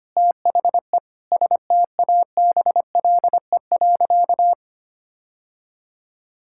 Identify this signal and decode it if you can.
Morse code, heard as a keyed tone, about 25 wpm, 700 Hz. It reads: THE STABLE.
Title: Sentences from Top 1600 Words 25wpm